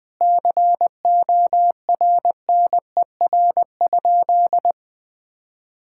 Morse code, heard as a keyed tone, about 21 wpm, 700 Hz. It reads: CORNER?
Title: Sentences from Top 1600 Words 20wpm